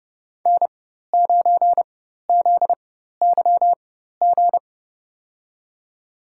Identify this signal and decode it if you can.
Morse code, heard as a keyed tone, about 32 wpm, spaced as for 14 wpm, 700 Hz. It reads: N9ZYG
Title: USA Call Signs - 5 Characters - 4x Character Spacing - 30wpm